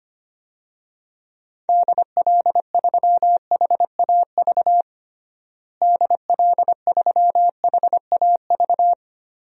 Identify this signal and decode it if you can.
Morse code, heard as a keyed tone, about 25 wpm, 700 Hz. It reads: DL3HAV DL3HAV
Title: DXCC-100 Call Signs - 6 Characters - Repeated 2x - 25wpm